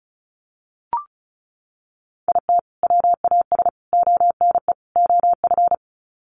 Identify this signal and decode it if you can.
Morse code, heard as a keyed tone, about 34 wpm, 700 Hz. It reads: IT WAS ONE OF
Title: Sets of 4 Words 35wpm